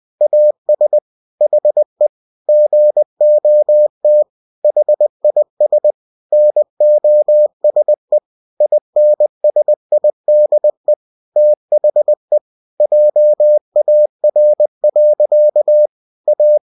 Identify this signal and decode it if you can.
Morse code, heard as a keyed tone, about 20 wpm, 600 Hz. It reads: AS HE GOT HIS NOSE INSIDE THE JAR. A
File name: cw-books / winnie-the-pooh / winnie-20wpm-page-0040